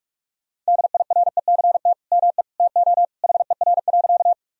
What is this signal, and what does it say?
Morse code, heard as a keyed tone, about 45 wpm, 700 Hz. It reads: DIRECT ME TO HER.
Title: Sentences from Top 1000 Words - Review Part 2 - 45wpm